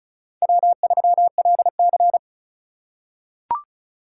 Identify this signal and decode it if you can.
Morse code, heard as a keyed tone, about 36 wpm, 690 Hz. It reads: W3LC
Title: USA Call Signs - 4 Characters - 2x Character Spacing - 35wpm